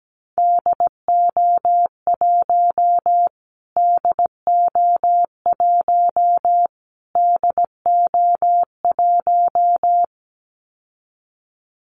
Morse code, heard as a keyed tone, about 17 wpm, 700 Hz. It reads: DO1 DO1 DO1
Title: Callsign Prefixes for Top 25 DXCC - Repeated 3x - 17wpm